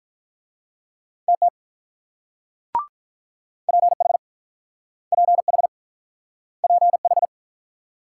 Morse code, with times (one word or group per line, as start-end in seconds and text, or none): 1.3-1.5 TT
3.7-4.2 PH
5.1-5.7 PH
6.6-7.3 PH